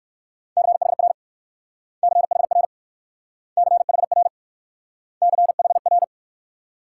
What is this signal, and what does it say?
Morse code, heard as a keyed tone, about 58 wpm, 700 Hz. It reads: CHR CHR CHR CHR